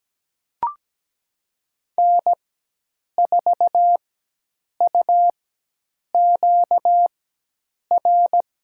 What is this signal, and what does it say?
Morse code, heard as a keyed tone, about 17 wpm, spaced as for 8 wpm, 700 Hz. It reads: N4UQR
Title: USA Call Signs - 5 Characters - 4x Character Spacing - 17wpm